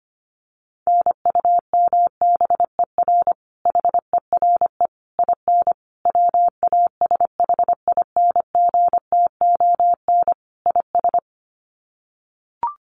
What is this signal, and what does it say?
Morse code, heard as a keyed tone, about 25 wpm, 700 Hz. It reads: NUMBER HERE IN WASHINGTON IS
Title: Sets of 5 Words Encore2 25wpm